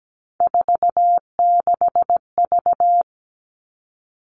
Morse code, heard as a keyed tone, about 17 wpm, 700 Hz. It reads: 46V
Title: All Cut Numbers - 3-Digit 17wpm